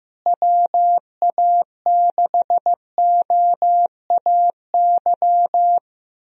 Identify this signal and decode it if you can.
Morse code, heard as a keyed tone, about 15 wpm, 700 Hz. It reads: WA6OAY
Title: USA Call Signs - 6 Characters - Repeated 2x - 15wpm